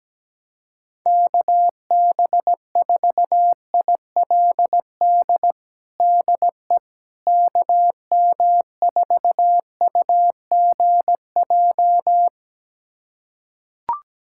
Morse code, heard as a keyed tone, about 17 wpm, 700 Hz. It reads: KB4ILD DE KM4UGJ
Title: USA Callsign DE Callsign - 6 Characters - 17wpm